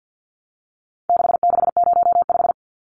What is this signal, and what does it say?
Morse code, heard as a keyed tone, about 50 wpm, 700 Hz. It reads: 6605
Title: Numbers - 4-Digit 50wpm